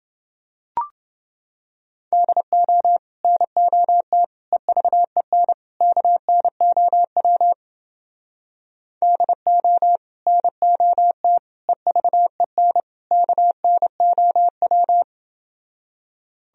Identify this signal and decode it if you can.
Morse code, heard as a keyed tone, about 29 wpm, 700 Hz. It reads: DO NOT EVEN KNOW DO NOT EVEN KNOW